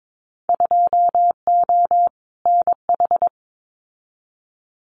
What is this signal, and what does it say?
Morse code, heard as a keyed tone, about 22 wpm, 700 Hz. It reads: 2O NH
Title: Winter Field Day Exchanges 22wpm